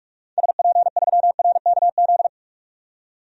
Morse code, heard as a keyed tone, about 46 wpm, 700 Hz. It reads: SP3RKZ
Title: DXCC-100 Call Signs - 6 Characters - 45wpm